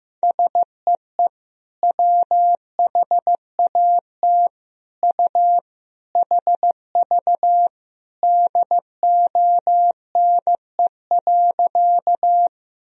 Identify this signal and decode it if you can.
Morse code, heard as a keyed tone, about 16 wpm, 700 Hz. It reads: SEE WHAT U HV DONE.
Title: Sentences from Top 500 Words and Abbreviations 15wpm